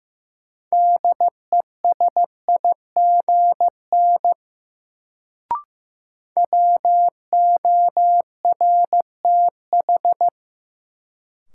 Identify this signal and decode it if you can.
Morse code, heard as a keyed tone, about 15 wpm, 700 Hz. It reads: DESIGN WORTH
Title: Sentences from Top 800 Words - New Word Drill - 15wpm